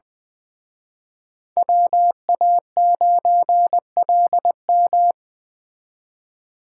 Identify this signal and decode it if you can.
Morse code, heard as a keyed tone, about 24 wpm, 700 Hz. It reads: WA9LM
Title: Speed Racing - USA Call Signs 20wpm